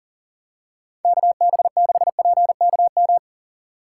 Morse code, heard as a keyed tone, about 40 wpm, 700 Hz. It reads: KB6PKM
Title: USA Call Signs - 6 Characters - 3x Character Spacing - 40wpm